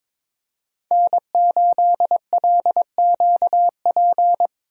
Morse code, heard as a keyed tone, about 22 wpm, 700 Hz. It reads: N8LQP
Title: USA Call Signs - 5 Characters - 3x Character Spacing - 22wpm